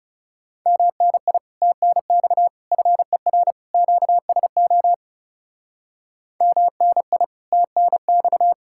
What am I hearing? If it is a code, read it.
Morse code, heard as a keyed tone, about 37 wpm, 700 Hz. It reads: MNI TNX FER QSO MNI TNX
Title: Speed-Racing Common QSO Phrases - 30wpm